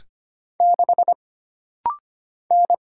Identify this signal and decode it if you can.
Morse code, heard as a keyed tone, about 25 wpm, 700 Hz. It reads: 6 N
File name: Single Character (Letter:Number:Punctuation:Prosign) - Rapid-Fire - 25wpm